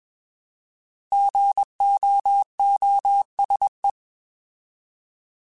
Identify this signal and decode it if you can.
Morse code, heard as a keyed tone, about 20 wpm, 800 Hz. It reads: GOOSE
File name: Alphabetical / Alphabetical mp3s / Gs @20